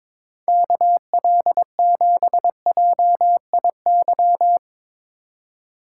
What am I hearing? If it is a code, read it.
Morse code, heard as a keyed tone, about 22 wpm, 700 Hz. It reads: KL7JIY
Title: USA Call Signs - 6 Characters - 2x Character Spacing - 22wpm